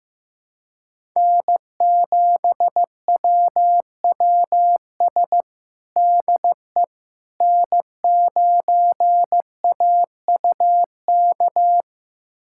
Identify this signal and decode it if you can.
Morse code, heard as a keyed tone, about 16 wpm, 700 Hz. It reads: N7WWS DE N9AUK
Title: USA Callsign DE Callsign - 5 Characters - 15wpm